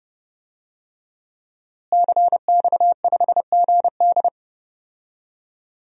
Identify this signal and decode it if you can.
Morse code, heard as a keyed tone, about 32 wpm, 700 Hz. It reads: CX5GD
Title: DXCC-100 Call Signs - 5 Characters - Repeated 2x - 30wpm